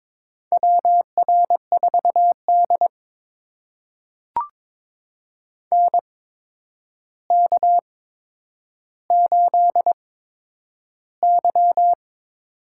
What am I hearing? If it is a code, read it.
Morse code, heard as a keyed tone, about 22 wpm, spaced as for 9 wpm, 700 Hz. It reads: WR4D NK8Y
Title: USA Call Signs - 4 Characters - 8x Character Spacing - 22wpm